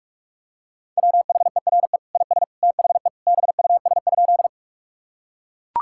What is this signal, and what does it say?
Morse code, heard as a keyed tone, about 44 wpm, 700 Hz. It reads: WHERE IS THE BUS?
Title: Sentences from Top 1000 Words - Review Part 6 - 45wpm